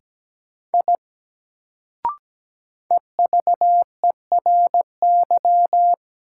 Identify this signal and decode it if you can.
Morse code, heard as a keyed tone, about 17 wpm, 700 Hz. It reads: I EVERY
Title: Sentences from Top 200 Words - New Word Drill - 17wpm